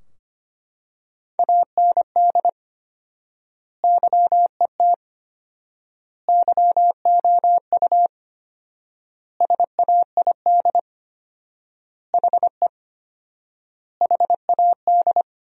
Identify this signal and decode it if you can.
Morse code, heard as a keyed tone, about 25 wpm, 700 Hz. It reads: AND YET YOU SAID HE HAD
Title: 4x Word-Spacing - Sentences from Top 500 Words - 25wpm